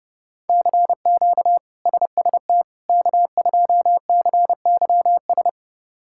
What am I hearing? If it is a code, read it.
Morse code, heard as a keyed tone, about 32 wpm, 700 Hz. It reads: CQ SST K2CYS
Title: K1USN SST QSO Practice - 30wpm